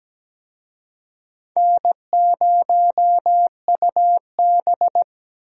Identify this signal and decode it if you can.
Morse code, heard as a keyed tone, about 17 wpm, 700 Hz. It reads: N0UB
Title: USA Call Signs - 4 Characters - Repeated 2x - 17wpm